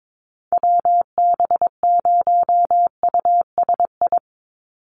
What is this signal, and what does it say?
Morse code, heard as a keyed tone, about 22 wpm, 700 Hz. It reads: WB0USI
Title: NA SOTA Chaser Callsigns 22wpm